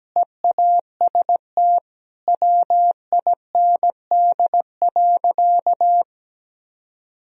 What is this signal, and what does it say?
Morse code, heard as a keyed tone, about 17 wpm, 700 Hz. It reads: EAST WIND.
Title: Sentences from Top 1000 Words - Review Part 3 - 17wpm